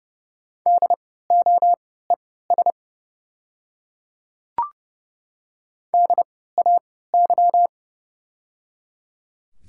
Morse code, heard as a keyed tone, about 30 wpm, spaced as for 17 wpm, 690 Hz. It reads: DOES DAY
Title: 3x Character Spacing - Top 300 Words - 30wpm